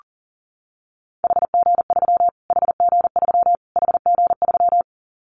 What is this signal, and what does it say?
Morse code, heard as a keyed tone, about 40 wpm, 700 Hz. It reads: HG3 HG3 HG3
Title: Callsign Prefixes for Top 25 DXCC - Repeated 3x - 40wpm